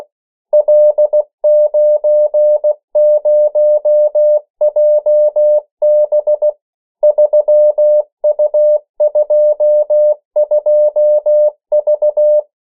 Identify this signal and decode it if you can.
Morse code, heard as a keyed tone, about 16 wpm, 600 Hz. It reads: L90JB 3U22V
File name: morse mp3 / 14 urok 13